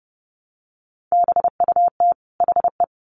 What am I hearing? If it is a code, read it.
Morse code, heard as a keyed tone, about 30 wpm, 700 Hz. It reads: BUT HE
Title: Sets of 2 Words 30wpm